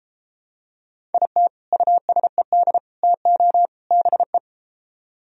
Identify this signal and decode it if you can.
Morse code, heard as a keyed tone, about 33 wpm, 700 Hz. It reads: IT USED TO BE